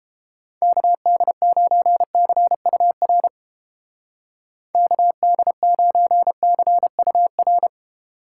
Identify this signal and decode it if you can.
Morse code, heard as a keyed tone, about 29 wpm, 700 Hz. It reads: KD9CUR KD9CUR